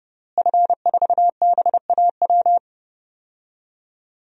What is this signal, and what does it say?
Morse code, heard as a keyed tone, about 32 wpm, 700 Hz. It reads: F4BAW
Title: DXCC-100 Call Signs - 5 Characters - Repeated 3x - 30wpm